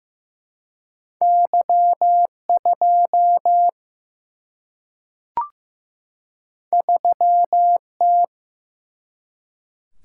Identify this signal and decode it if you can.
Morse code, heard as a keyed tone, about 15 wpm, 700 Hz. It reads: Y2 3T
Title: Single-Letter Single-Number 15wpm